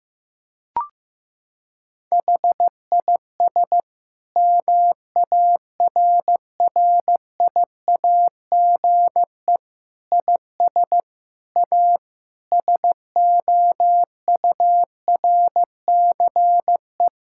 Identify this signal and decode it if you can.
Morse code, heard as a keyed tone, about 15 wpm, 700 Hz. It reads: HIS MARRIAGE IS A SOURCE
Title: Sentences from Top 1000 Words - Review Part 6 - 15wpm